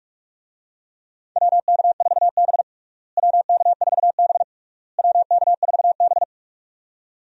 Morse code, heard as a keyed tone, about 47 wpm, 700 Hz. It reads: WK4B WK4B WK4B
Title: USA Call Signs - 4 Characters - Repeated 3x - 45wpm